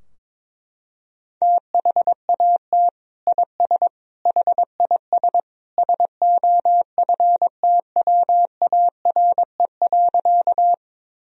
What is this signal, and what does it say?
Morse code, heard as a keyed tone, about 22 wpm, 700 Hz. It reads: THAT IS HIS SOFTWARE.
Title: Sentences from Top 1000 Words - Review Part 3 - 22wpm